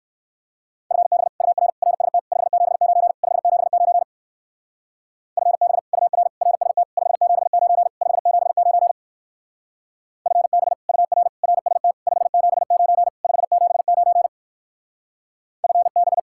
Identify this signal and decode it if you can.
Morse code, heard as a keyed tone, about 60 wpm, 700 Hz. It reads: FB UR RST 579 579 FB UR RST 579 579 FB UR RST 579 579 FB